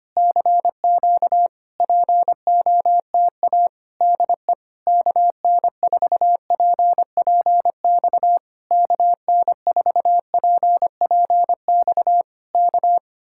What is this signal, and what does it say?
Morse code, heard as a keyed tone, about 26 wpm, 700 Hz. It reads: CQ POTA DE KN4PPX KN4PPX K
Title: Speed-Racing Common QSO Phrases Encore - 25wpm